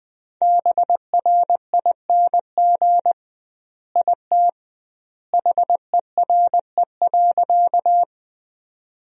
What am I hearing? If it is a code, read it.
Morse code, heard as a keyed tone, about 21 wpm, 700 Hz. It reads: BRING IT HERE.
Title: 2x Word-Spacing - Sentences from Top 500 Words - 20wpm